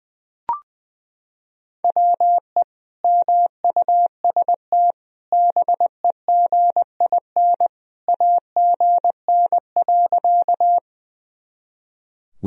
Morse code, heard as a keyed tone, about 20 wpm, 700 Hz. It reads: WE MUST BEGIN AGN.
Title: Sentences from Top 300 Words and Abbreviations 20wpm